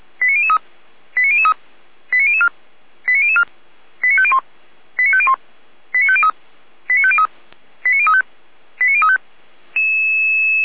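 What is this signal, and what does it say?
Five-tone selective call (ZVEI), heard as quick runs of five tones, tones of about 70 ms. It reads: 89003 89003 89004 89004 89501 89501 89503 89503 89035 89035
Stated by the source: ZVEI five-tone selective call 89001 received by SDR.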